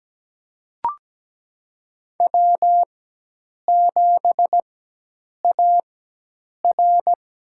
Morse code, heard as a keyed tone, about 17 wpm, spaced as for 8 wpm, 700 Hz. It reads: W7AR
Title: USA Call Signs - 4x Character Spacing - 17wpm